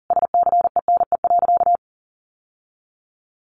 Morse code, heard as a keyed tone, about 40 wpm, 700 Hz. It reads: SCENE.
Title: Sentences from Top 1100 Words 40wpm